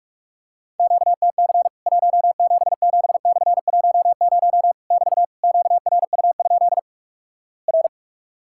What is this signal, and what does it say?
Morse code, heard as a keyed tone, about 46 wpm, 700 Hz. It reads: QTC 177/10 <BT> QRU? R
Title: Worked All Europe DX Contest - QTC Exchange - 45wpm